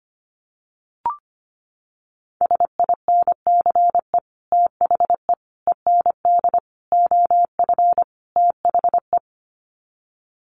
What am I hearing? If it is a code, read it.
Morse code, heard as a keyed tone, about 25 wpm, 700 Hz. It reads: SINCE THE END OF THE
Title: Sets of 5 Words Encore 25wpm